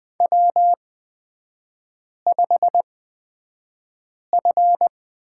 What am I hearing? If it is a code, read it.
Morse code, heard as a keyed tone, about 20 wpm, spaced as for 5 wpm, 700 Hz. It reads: W5F
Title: 243 Lesson 39e - Letter z - Three Characters 20wpm